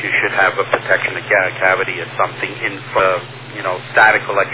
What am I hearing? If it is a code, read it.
Emergency Alert System (SAME) tones, heard as harsh buzzing bursts, with speech. It is heard from 0.8 to 4.2 s.